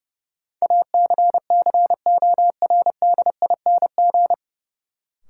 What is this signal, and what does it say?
Morse code, heard as a keyed tone, about 29 wpm, 700 Hz. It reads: ACCORDING
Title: Sentences from Top 700 Words - New Word Drill - 30wpm